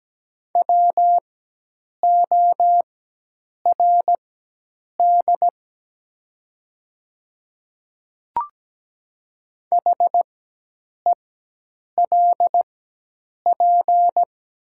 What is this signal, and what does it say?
Morse code, heard as a keyed tone, about 17 wpm, spaced as for 8 wpm, 700 Hz. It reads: WORD HELP
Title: 4x Character Spacing - Top 200 Words - 17wpm